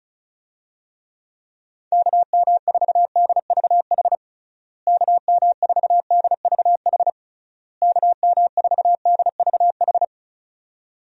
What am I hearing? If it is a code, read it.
Morse code, heard as a keyed tone, about 36 wpm, 700 Hz. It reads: KM4DVH KM4DVH KM4DVH
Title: USA Call Signs - 6 Characters - Repeated 3x - 35wpm